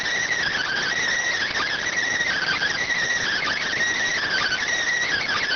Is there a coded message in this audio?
Yes: Emergency Alert System (SAME) tones from 0.5 to 4.6 s, heard as harsh buzzing bursts.